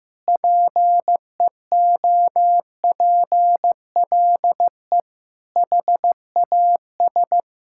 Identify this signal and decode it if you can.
Morse code, heard as a keyed tone, about 15 wpm, 700 Hz. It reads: PEOPLE HAS
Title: Sentences from Top 1000 Words - Review Part 6 - 15wpm